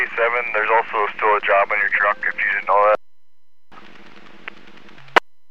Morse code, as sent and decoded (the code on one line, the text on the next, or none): none